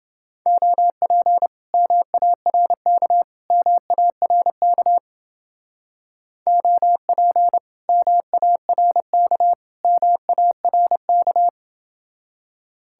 Morse code, heard as a keyed tone, about 28 wpm, 700 Hz. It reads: OP MARK MARK OP MARK MARK